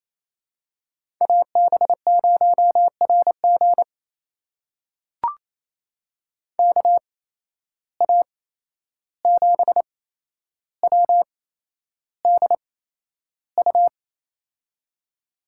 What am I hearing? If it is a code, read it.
Morse code, heard as a keyed tone, about 28 wpm, spaced as for 11 wpm, 700 Hz. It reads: AB0RG KA7WDU
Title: USA Call Signs - 8x Character Spacing - 28wpm